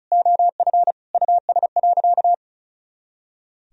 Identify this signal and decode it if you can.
Morse code, heard as a keyed tone, about 36 wpm, 700 Hz. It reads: OF US.
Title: Sentences from Top 300 Words and Abbreviations 35wpm